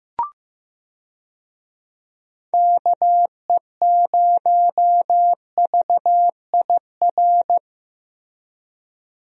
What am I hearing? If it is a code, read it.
Morse code, heard as a keyed tone, about 15 wpm, 700 Hz. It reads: KE0VIR
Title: USA Call Signs - 6 Characters - Repeated 3x - 15wpm